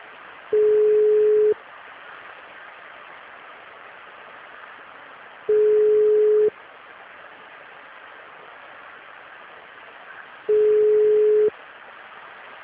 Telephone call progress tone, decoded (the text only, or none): ringback tone